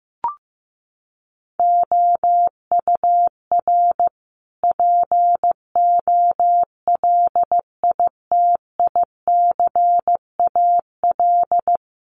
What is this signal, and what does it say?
Morse code, heard as a keyed tone, about 15 wpm, 700 Hz. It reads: OUR POLITICAL
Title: Sentences from Top 900 Words 15wpm